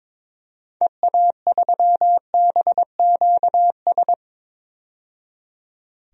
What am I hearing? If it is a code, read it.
Morse code, heard as a keyed tone, about 22 wpm, 700 Hz. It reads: EA3BQS
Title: DXCC-100 Call Signs - 6 Characters - 22wpm